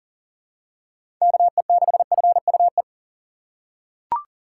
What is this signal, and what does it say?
Morse code, heard as a keyed tone, about 40 wpm, 700 Hz. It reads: KE6FUE
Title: USA Call Signs - 6 Characters - 6x Character Spacing - 40wpm